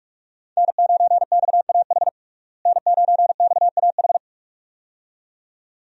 Morse code, heard as a keyed tone, about 46 wpm, 700 Hz. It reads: N9XAH N9XAH
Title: USA Call Signs - 5 Characters - Repeated 2x - 45wpm